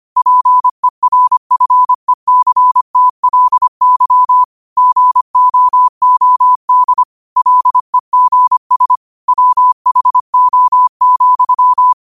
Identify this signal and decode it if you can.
Morse code, heard as a keyed tone, about 25 wpm, 1000 Hz. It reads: PERFECTLY GOOD LEGS WHO,
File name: 25wpm